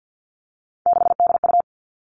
Morse code, heard as a keyed tone, about 50 wpm, 700 Hz. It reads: 6DU
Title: Three Challenging Letter-Number Combinations 50wpm